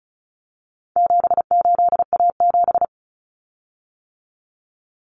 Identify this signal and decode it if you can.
Morse code, heard as a keyed tone, about 34 wpm, 700 Hz. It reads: 78A7
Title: Common Cut Numbers - 4-Digit 35wpm